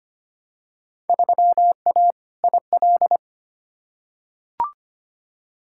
Morse code, heard as a keyed tone, about 25 wpm, 700 Hz. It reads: 3A IL
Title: Field Day Exchanges - 25wpm